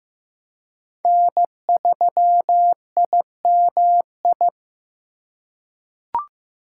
Morse code, heard as a keyed tone, about 15 wpm, 700 Hz. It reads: N3IMI
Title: USA Call Signs - 5 Characters - 3x Character Spacing - 15wpm